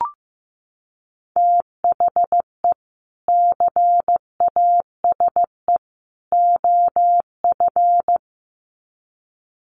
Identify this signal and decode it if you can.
Morse code, heard as a keyed tone, about 15 wpm, 700 Hz. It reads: THE CASE OF
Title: Sets of 3 Words 15wpm